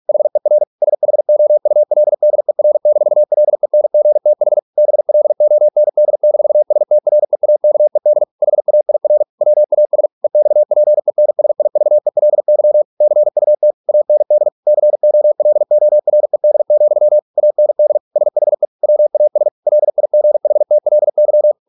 Morse code, heard as a keyed tone, about 46 wpm, 600 Hz. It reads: HER SHOULDER-LENGTH BLOND-STREAKED HAIR WAS EXPENSIVELY CUT AND COLORED, AND SHE WAS LIGHTLY